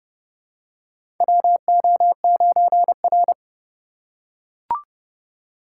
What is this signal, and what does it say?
Morse code, heard as a keyed tone, about 30 wpm, 700 Hz. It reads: WO9R
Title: USA Call Signs - 4 Characters - 2x Character Spacing - 30wpm